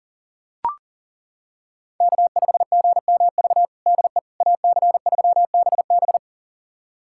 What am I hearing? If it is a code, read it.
Morse code, heard as a keyed tone, about 40 wpm, 700 Hz. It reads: K5GMV DE AC3BB
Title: USA Callsign DE Callsign - 5 Characters - 40wpm